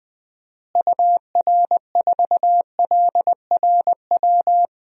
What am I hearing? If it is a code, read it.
Morse code, heard as a keyed tone, about 20 wpm, 700 Hz. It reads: UR4LRW
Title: DXCC-100 Call Signs - 6 Characters - Repeated 3x - 20wpm